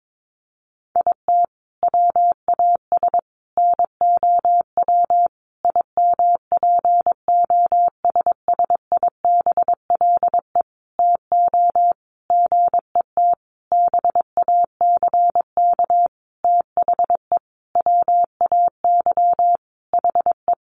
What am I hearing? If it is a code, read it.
Morse code, heard as a keyed tone, about 22 wpm, 700 Hz. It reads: IT WAS NOW IMPOSSIBLE TO GET BACK THE WAY HE
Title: Sentences from Top 1100 Words 22wpm